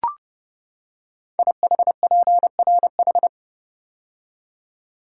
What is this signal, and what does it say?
Morse code, heard as a keyed tone, about 30 wpm, 700 Hz. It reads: IHPRH
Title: Code Groups - 5 Letters - 30wpm